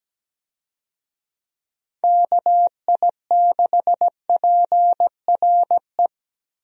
Morse code, heard as a keyed tone, about 17 wpm, 700 Hz. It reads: KI6PRE
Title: USA Call Signs - 6 Characters - 17wpm